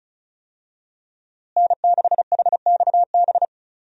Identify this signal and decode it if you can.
Morse code, heard as a keyed tone, about 36 wpm, 700 Hz. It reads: N6HXB
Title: USA Call Signs - 5 Characters - Repeated 3x - 35wpm